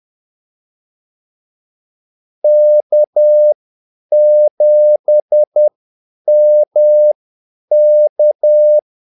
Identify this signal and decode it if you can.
Morse code, heard as a keyed tone, about 10 wpm, spaced as for 8 wpm, 600 Hz. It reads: K7MK
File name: Callsigns-10-8wpm